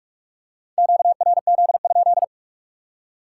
Morse code, heard as a keyed tone, about 44 wpm, 700 Hz. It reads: QRZ?
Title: Top 180 QSO Elements 45wpm